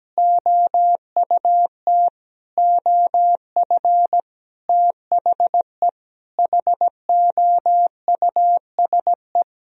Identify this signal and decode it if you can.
Morse code, heard as a keyed tone, about 17 wpm, 700 Hz. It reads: OUT OF THE HOUSE
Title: Sentences from Top 1000 Words - Review Part 4 - 17wpm